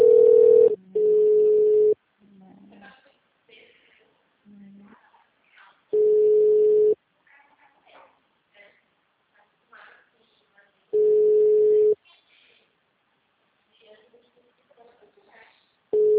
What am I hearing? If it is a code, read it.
Ringback tone (Europe, 425 Hz), with speech.
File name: out- -2018- - - .2